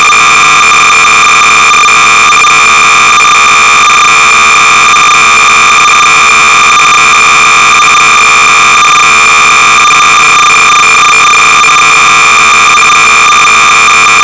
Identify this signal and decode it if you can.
Cassette data (Kansas City type).
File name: sorcerer / 300baud / XNIM-(1980)-(Wollongong-Sorcerer-Users-Group)-BASIC-Sorcerer-300baud